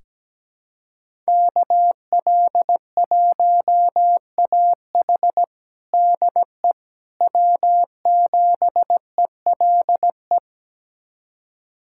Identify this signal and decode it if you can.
Morse code, heard as a keyed tone, about 17 wpm, 700 Hz. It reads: KL1AH DE W7ELE
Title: USA Callsign DE Callsign 17wpm